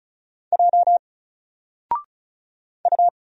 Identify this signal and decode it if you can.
Morse code, heard as a keyed tone, about 34 wpm, 700 Hz. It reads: J U
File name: Single-Letter-Number-35wpm